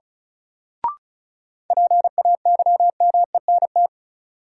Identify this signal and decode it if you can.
Morse code, heard as a keyed tone, about 36 wpm, 700 Hz. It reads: PAYMENT
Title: Sentences from Top 1200 Words - New Word Drill Alt - 35wpm